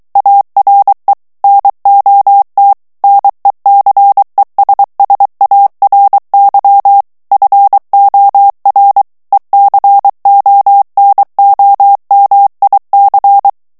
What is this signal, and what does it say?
Morse code, heard as a keyed tone, about 23 wpm, 800 Hz. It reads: ARE NOT NECESSARY FOR ECONOMIC
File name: morse mp3 / ANGLAIS texte clair 1 vitesse 22 mots minute